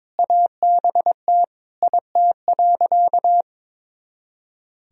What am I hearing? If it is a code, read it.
Morse code, heard as a keyed tone, about 21 wpm, 700 Hz. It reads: ABT IT.
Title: Sentences from Top 500 Words and Abbreviations 22wpm